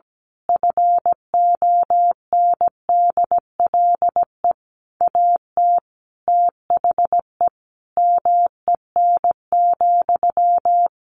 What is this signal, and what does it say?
Morse code, heard as a keyed tone, about 17 wpm, 700 Hz. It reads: FONDLE AT THE MEN,
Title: A Sound Of Thunder 17wpm